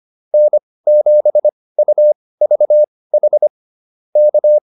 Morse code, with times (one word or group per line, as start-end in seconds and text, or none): 0.3-3.5 N7UVH
4.2-4.6 K